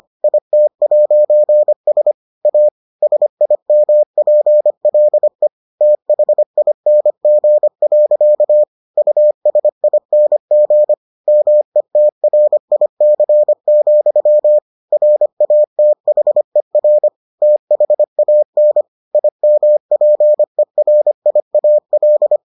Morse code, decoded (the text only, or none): IT'S A SIMPLE THING. USING METRIC, RATHER THAN IMPERIAL